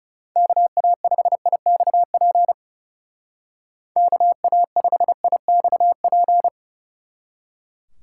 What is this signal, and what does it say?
Morse code, heard as a keyed tone, about 36 wpm, 700 Hz. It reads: KA5IXP KA5IXP
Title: Speed Racing - USA 2x3 Call Signs 30wpm